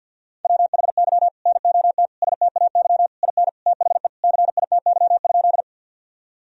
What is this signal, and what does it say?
Morse code, heard as a keyed tone, about 48 wpm, 700 Hz. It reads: WHY NOT STAY IN THE CITY?
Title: Sentences from Top 1000 Words - Review Part 1 - 50wpm